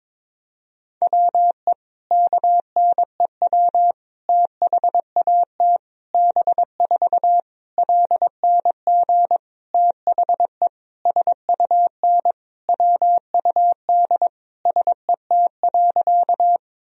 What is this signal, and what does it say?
Morse code, heard as a keyed tone, about 22 wpm, 700 Hz. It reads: WE KNEW THAT B4 LNG THE SUN WUD SET.
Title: Sentences from Top 400 Words and Abbreviations 22wpm